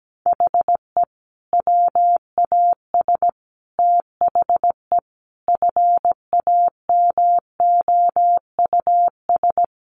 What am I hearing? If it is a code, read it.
Morse code, heard as a keyed tone, about 17 wpm, 700 Hz. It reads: HE WAS THE FAMOUS
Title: Sentences from Top 1500 Words 17wpm